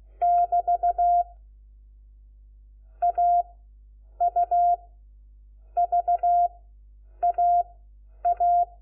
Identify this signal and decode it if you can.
Morse code, heard as a keyed tone, about 16 wpm, spaced as for 8 wpm, 700 Hz. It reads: <BT> AUVAA
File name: Lecciones / CLASE Nº4